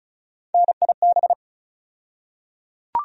Morse code, heard as a keyed tone, about 36 wpm, 700 Hz. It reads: NIB
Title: USA Call Sign Postfixes - 3 Letters 35wpm